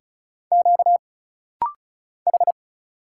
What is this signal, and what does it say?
Morse code, heard as a keyed tone, about 36 wpm, 700 Hz. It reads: Q H
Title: Single Letters - Rapid-Fire 35wpm